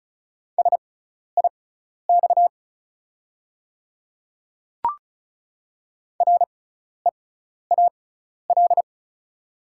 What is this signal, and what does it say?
Morse code, heard as a keyed tone, about 36 wpm, spaced as for 12 wpm, 690 Hz. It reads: SIX REAL
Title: 6x Character Spacing - Top 500 Words - 35wpm